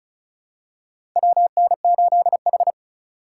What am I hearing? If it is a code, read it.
Morse code, heard as a keyed tone, about 34 wpm, 700 Hz. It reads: WN8H
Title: USA Call Signs - 4 Characters - 2x Character Spacing - 35wpm